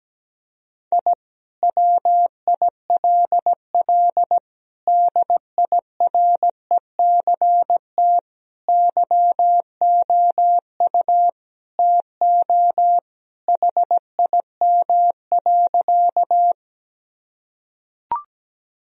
Morse code, heard as a keyed tone, about 17 wpm, 700 Hz. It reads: I WILL DIRECT YOU TO HIM.
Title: Sentences from Top 1000 Words - Review Part 2 - 17wpm